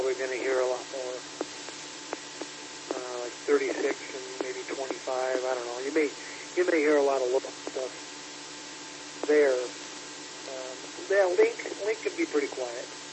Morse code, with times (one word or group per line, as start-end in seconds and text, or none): none